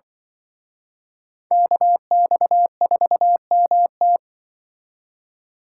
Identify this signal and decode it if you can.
Morse code, heard as a keyed tone, about 25 wpm, 700 Hz. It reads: KX4MT
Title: Speed Racing - USA 2x2 Call Signs 20wpm